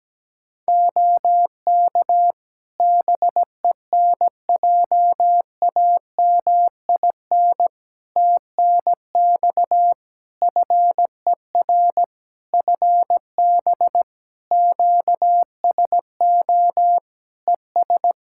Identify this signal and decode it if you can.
Morse code, heard as a keyed tone, about 17 wpm, 700 Hz. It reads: OK BENJAMIN TNX FER FB QSO ES
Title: Practice QSO - Exchange 4 - Ending - 17wpm